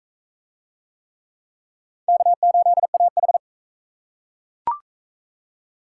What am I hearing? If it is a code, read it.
Morse code, heard as a keyed tone, about 41 wpm, 700 Hz. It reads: K8AH